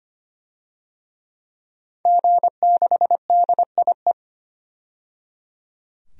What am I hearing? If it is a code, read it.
Morse code, heard as a keyed tone, about 25 wpm, 700 Hz. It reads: G6DIE
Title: DXCC-100 Call Signs - 5 Characters - 25wpm